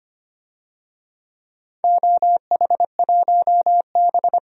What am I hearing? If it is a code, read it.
Morse code, heard as a keyed tone, about 25 wpm, 700 Hz. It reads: OH1B
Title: DXCC-100 Call Signs - 4 Characters - 25wpm